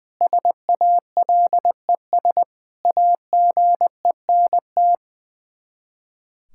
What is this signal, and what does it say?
Morse code, heard as a keyed tone, about 21 wpm, 700 Hz. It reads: SALES AGENT
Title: Practice QSO - Exchange 3 - More Info - 20wpm